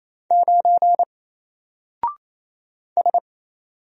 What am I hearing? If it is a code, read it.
Morse code, heard as a keyed tone, about 28 wpm, 700 Hz. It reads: 9 S
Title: Single Letter-Number Alt 28wpm